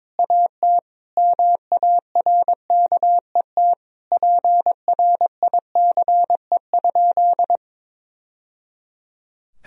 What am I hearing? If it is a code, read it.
Morse code, heard as a keyed tone, about 22 wpm, 700 Hz. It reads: AT MARKET PRICE?
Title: Sentences from Top 1000 Words - Review Part 3 - 22wpm